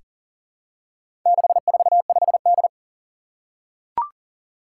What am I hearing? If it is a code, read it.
Morse code, heard as a keyed tone, about 38 wpm, 700 Hz. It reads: 645D
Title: Four Challenging Letter-Number Combinations 40wpm